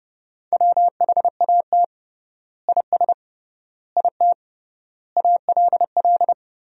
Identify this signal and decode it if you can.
Morse code, heard as a keyed tone, about 30 wpm, 700 Hz. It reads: WHAT IS IT ALL
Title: 3x Word-Spacing - Sentences from Top 100 Words - 30wpm